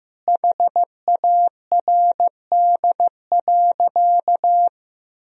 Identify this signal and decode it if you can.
Morse code, heard as a keyed tone, about 15 wpm, 700 Hz. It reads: HARD.
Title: Sentences from Top 1000 Words - Review Part 4 - 15wpm